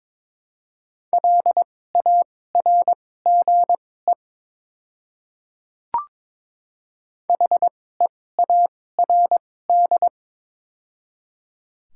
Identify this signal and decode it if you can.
Morse code, heard as a keyed tone, about 22 wpm, spaced as for 16 wpm, 700 Hz. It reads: LARGE HEARD
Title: 2x Character Spacing - Top 500 Words - 22wpm